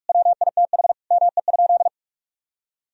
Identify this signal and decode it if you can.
Morse code, heard as a keyed tone, about 41 wpm, 700 Hz. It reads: WITH ME?
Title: Sentences from Top 1000 Words - Review Part 1 - 45wpm